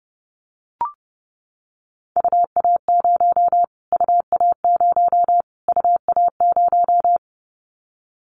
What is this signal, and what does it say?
Morse code, heard as a keyed tone, about 30 wpm, 690 Hz. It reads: UA0 UA0 UA0
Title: Callsign Prefixes for Top 25 DXCC - Repeated 3x - 30wpm